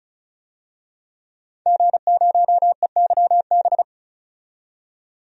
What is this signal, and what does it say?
Morse code, heard as a keyed tone, about 36 wpm, 700 Hz. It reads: G0EYB
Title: DXCC-100 Call Signs - 5 Characters - 35wpm